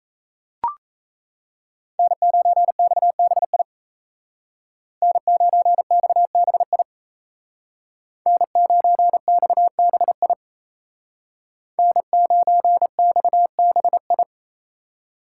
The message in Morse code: N9XBI N9XBI N9XBI N9XBI

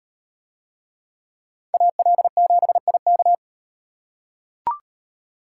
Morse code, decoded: AL7IK

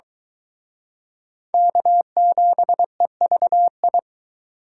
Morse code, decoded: K7EVI

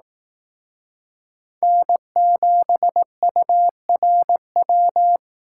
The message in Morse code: N7URW

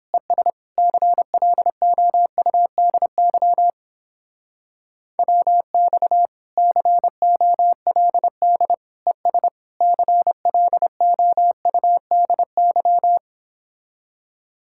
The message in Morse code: ES CLOUDY WX COLD ES CLOUDY

